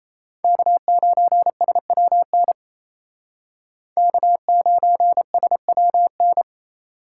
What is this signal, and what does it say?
Morse code, heard as a keyed tone, about 36 wpm, 700 Hz. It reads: K9SWN K9SWN